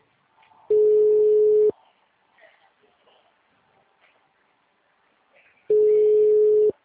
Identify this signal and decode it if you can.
Ringback tone (Europe, 425 Hz), with speech.
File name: out- -2011- - - .25